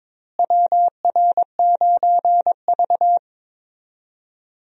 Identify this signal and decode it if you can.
Morse code, heard as a keyed tone, about 21 wpm, 700 Hz. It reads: WR9V
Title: USA Call Signs - 4 Characters - Repeated 3x - 22wpm